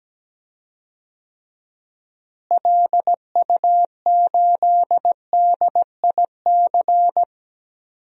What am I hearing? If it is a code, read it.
Morse code, heard as a keyed tone, about 17 wpm, 700 Hz. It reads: LU8DIC
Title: DXCC-100 Call Signs - 6 Characters - Repeated 2x - 17wpm